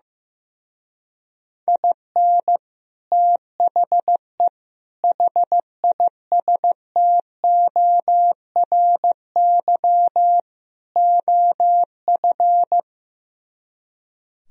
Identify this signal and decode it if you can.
Morse code, heard as a keyed tone, about 15 wpm, 700 Hz. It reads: IN THE HISTORY OF
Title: Speed-Racing - Sets of 4 Words 15wpm